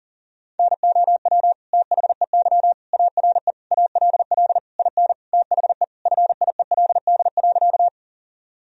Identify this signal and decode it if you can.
Morse code, heard as a keyed tone, about 37 wpm, 700 Hz. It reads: NOW THEY ARE ALL IN THE FIELD.
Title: Sentences from Top 1000 Words - Review Part 2 - 40wpm